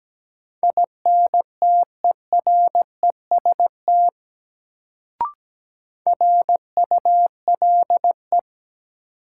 Morse code, heard as a keyed tone, about 17 wpm, 700 Hz. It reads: INTEREST RULE
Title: Sentences from Top 400 Words - New Word Drill - 17wpm